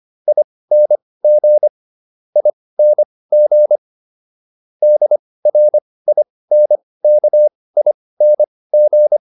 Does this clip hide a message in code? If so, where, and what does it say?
Yes: Morse code from 0.3 to 9.2 s, heard as a keyed tone, about 25 wpm, spaced as for 18 wpm, 600 Hz. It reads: ING ING DRINKING